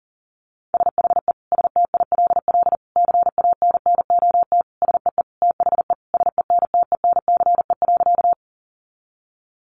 Morse code, heard as a keyed tone, about 40 wpm, 700 Hz. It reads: SHE STILL CANNOT SEE THE SENTENCE.
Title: Additional Easy Phrases 40wpm